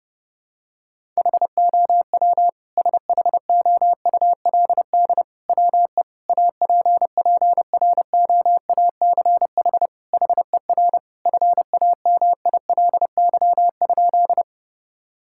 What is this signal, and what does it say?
Morse code, heard as a keyed tone, about 32 wpm, 700 Hz. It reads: HOW SHOULD WE APPROACH HER FAMILY?
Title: Sentences from Top 1000 Words - Review Part 6 - 30wpm